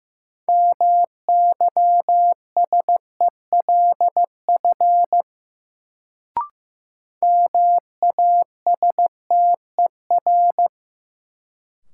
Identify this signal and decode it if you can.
Morse code, heard as a keyed tone, about 15 wpm, 700 Hz. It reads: MYSELF MASTER
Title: Sentences from Top 1300 Words - New Word Drill - 15wpm